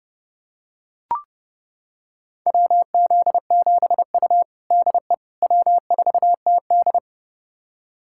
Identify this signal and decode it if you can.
Morse code, heard as a keyed tone, about 30 wpm, 700 Hz. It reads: WZ7U DE W4TD
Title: USA Callsign DE Callsign - 4 Characters - 30wpm